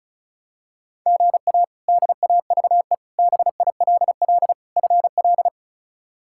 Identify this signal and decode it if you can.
Morse code, heard as a keyed tone, about 34 wpm, 700 Hz. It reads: GA DAVE BILL FL
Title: K1USN SST QSO Practice - 35wpm